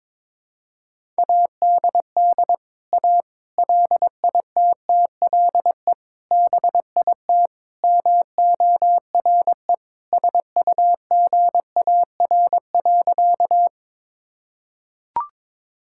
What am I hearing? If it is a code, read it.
Morse code, heard as a keyed tone, about 22 wpm, 700 Hz. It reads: ADD A LITTLE BIT MORE SUGAR.
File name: practice / 22wpm / Sentences with 4-Word Phrases - 22wpm